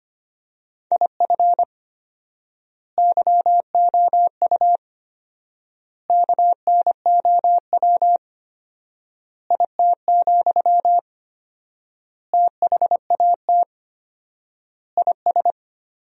Morse code, heard as a keyed tone, about 25 wpm, 700 Hz. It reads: IF YOU KNOW IT, THAT IS
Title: 4x Word-Spacing - Sentences from Top 400 Words - 25wpm